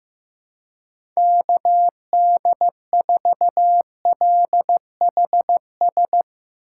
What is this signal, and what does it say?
Morse code, heard as a keyed tone, about 15 wpm, 700 Hz. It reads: KD4LHS
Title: Speed Racing - USA 2x3 Call Signs 15wpm